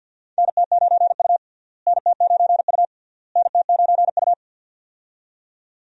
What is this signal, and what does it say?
Morse code, heard as a keyed tone, about 52 wpm, 700 Hz. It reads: NT9U NT9U NT9U
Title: USA Call Signs - 4 Characters - Repeated 3x - 50wpm